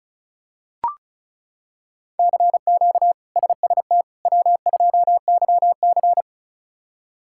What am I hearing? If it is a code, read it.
Morse code, heard as a keyed tone, about 34 wpm, 700 Hz. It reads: CQ SST W2YC
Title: K1USN SST QSO Practice - 35wpm